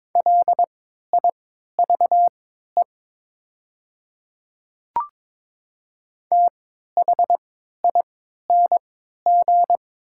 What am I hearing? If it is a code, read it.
Morse code, heard as a keyed tone, about 22 wpm, spaced as for 13 wpm, 700 Hz. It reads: LIVE THING
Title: 3x Character Spacing - Top 200 Words - 22wpm